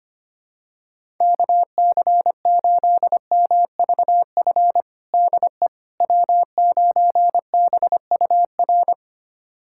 Morse code, heard as a keyed tone, about 25 wpm, 700 Hz. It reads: KC8MVF DE W9BUR
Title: USA Callsign DE Callsign 25wpm